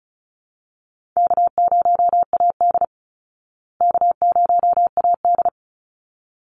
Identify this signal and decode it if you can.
Morse code, heard as a keyed tone, about 34 wpm, 700 Hz. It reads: K0AD K0AD
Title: Easy Call Signs Repeated 2x - 35wpm